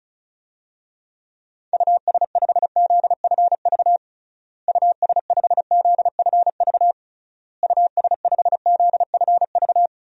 Morse code, heard as a keyed tone, about 36 wpm, 700 Hz. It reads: US5ZFV US5ZFV US5ZFV
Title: DXCC-100 Call Signs - 6 Characters - Repeated 3x - 35wpm